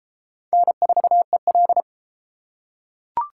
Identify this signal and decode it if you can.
Morse code, heard as a keyed tone, about 31 wpm, 700 Hz. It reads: N4EL